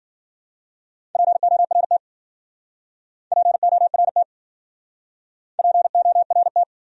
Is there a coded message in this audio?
Yes: Morse code from 1.2 to 6.6 s, heard as a keyed tone, about 52 wpm, 700 Hz. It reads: PORT PORT PORT